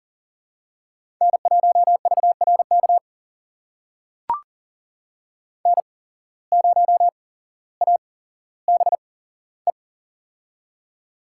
Morse code, heard as a keyed tone, about 41 wpm, spaced as for 17 wpm, 700 Hz. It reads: N1VRK N0ABE
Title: USA Call Signs - 5 Characters - 8x Character Spacing - 40wpm